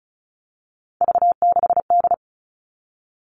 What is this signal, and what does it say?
Morse code, heard as a keyed tone, about 34 wpm, 700 Hz. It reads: V6D
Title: All Cut Numbers - 3-Digit 35wpm